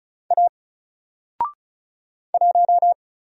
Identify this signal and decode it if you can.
Morse code, heard as a keyed tone, about 36 wpm, 700 Hz. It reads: A 1
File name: Single-Letter-Number-35wpm